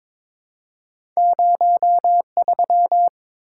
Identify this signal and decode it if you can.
Morse code, heard as a keyed tone, about 22 wpm, 700 Hz. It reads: 03
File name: practice / 22wpm / Two Letter-Number Combinations - 22wpm